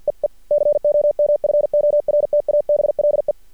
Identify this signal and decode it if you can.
Morse code, heard as a keyed tone, about 50 wpm, 590 Hz. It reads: EE COMFORTABLE
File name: archived / CRSnet